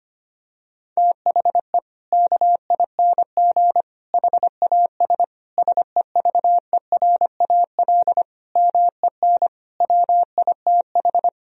Morse code, heard as a keyed tone, about 25 wpm, 700 Hz. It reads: THE KING HAS SEVERAL MEN WITH
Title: Sentences from Top 1000 Words - Review Part 2 - 25wpm